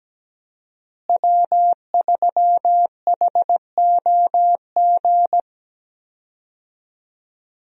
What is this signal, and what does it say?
Morse code, heard as a keyed tone, about 17 wpm, 700 Hz. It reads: W3HOG
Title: USA Call Signs - 5 Characters - Repeated 3x - 17wpm